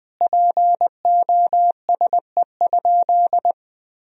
Morse code, heard as a keyed tone, about 21 wpm, 700 Hz. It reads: POSE?
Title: Sentences from Top 1000 Words - Review Part 2 - 20wpm